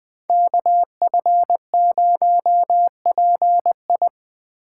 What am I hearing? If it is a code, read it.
Morse code, heard as a keyed tone, about 20 wpm, 700 Hz. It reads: KF0PI
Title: USA Call Signs - 5 Characters - 3x Character Spacing - 20wpm